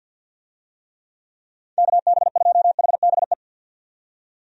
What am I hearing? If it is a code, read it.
Morse code, heard as a keyed tone, about 46 wpm, 700 Hz. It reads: KB2HBE
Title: USA Call Signs - 6 Characters - 50wpm